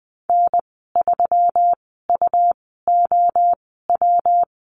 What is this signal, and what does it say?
Morse code, heard as a keyed tone, about 20 wpm, spaced as for 15 wpm, 700 Hz. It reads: N3UOW
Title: 122 Lesson 20g - Number 3 - Call Signs 20wpm